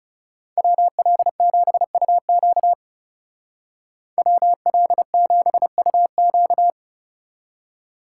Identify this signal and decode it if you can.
Morse code, heard as a keyed tone, about 35 wpm, 700 Hz. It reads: WL7UQ WL7UQ
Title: Speed Racing - USA 2x2 Call Signs 30wpm